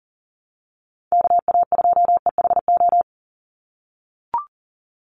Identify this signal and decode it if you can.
Morse code, heard as a keyed tone, about 40 wpm, 700 Hz. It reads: KA2EHO
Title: USA Call Signs 40wpm